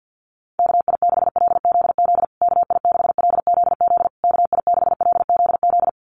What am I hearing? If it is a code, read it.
Morse code, heard as a keyed tone, about 50 wpm, 700 Hz. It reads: KI6LZZ KI6LZZ KI6LZZ
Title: USA Call Signs Repeated 3x 50wpm